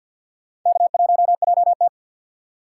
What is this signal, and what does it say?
Morse code, heard as a keyed tone, about 50 wpm, 690 Hz. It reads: K1JT
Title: USA Call Signs - 4x Character Spacing - 50wpm